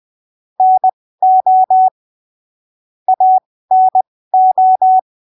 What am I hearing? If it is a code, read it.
Morse code, heard as a keyed tone, about 20 wpm, spaced as for 15 wpm, 750 Hz. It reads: NO ANO
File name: arquivos / texto / texto 17 20ppm